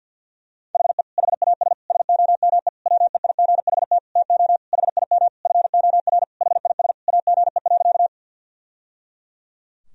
Morse code, heard as a keyed tone, about 50 wpm, 700 Hz. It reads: HE HAS SOME WEIGHT TO HIM FOR HIS AGE.
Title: Sentences from Top 1000 Words - Review Part 3 - 50wpm